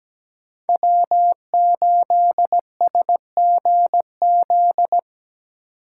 Morse code, heard as a keyed tone, about 17 wpm, 700 Hz. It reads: W8SGZ
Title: USA Call Signs - 4x Character Spacing - 17wpm